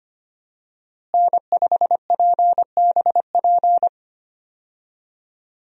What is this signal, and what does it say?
Morse code, heard as a keyed tone, about 25 wpm, 700 Hz. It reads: N5PBP
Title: USA Call Signs - 5 Characters - Repeated 3x - 25wpm